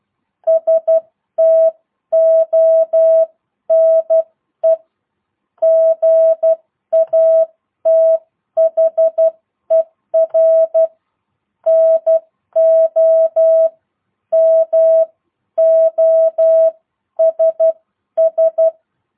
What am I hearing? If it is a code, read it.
Morse code, heard as a keyed tone, about 12 wpm, spaced as for 10 wpm, 660 Hz. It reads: STONE GATHER NO MOSS